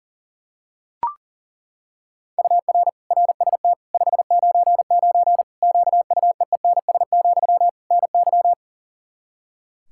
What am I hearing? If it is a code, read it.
Morse code, heard as a keyed tone, about 40 wpm, 700 Hz. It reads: UR RST 599 QUEENS, NY
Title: Practice QSO - Exchange 1 - Beginning - 40wpm